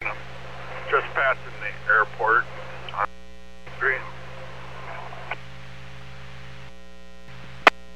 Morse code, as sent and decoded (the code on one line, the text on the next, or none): none